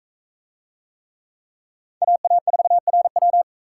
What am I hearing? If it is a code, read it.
Morse code, heard as a keyed tone, about 44 wpm, 700 Hz. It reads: AA4RW